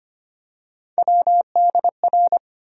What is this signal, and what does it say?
Morse code, heard as a keyed tone, about 25 wpm, 700 Hz. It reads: WDR
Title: USA Call Sign Postfixes - 3 Letters 25wpm